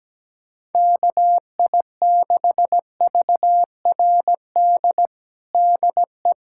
Morse code, heard as a keyed tone, about 17 wpm, 700 Hz. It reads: KI6VRD DE
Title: USA Callsign DE Callsign 17wpm